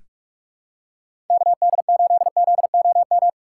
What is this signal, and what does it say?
Morse code, heard as a keyed tone, about 45 wpm, 700 Hz. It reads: KD8ZOM
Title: USA Call Signs - 6 Characters - 2x Character Spacing - 45wpm